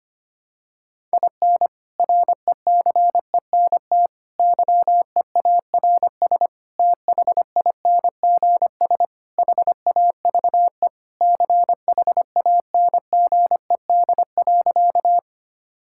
Morse code, heard as a keyed tone, about 25 wpm, 700 Hz. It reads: IN RECENT YEARS THINGS HAVE CHANGED.
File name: practice / 25wpm / Sentences with 3-Word Phrases - 25wpm